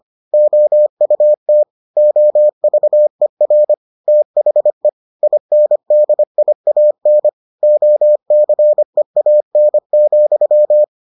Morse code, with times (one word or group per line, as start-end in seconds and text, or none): 0.3-1.6 OUT
2.0-3.7 OVER
4.1-4.9 THE
5.2-7.3 INDIAN
7.6-10.8 OCEAN,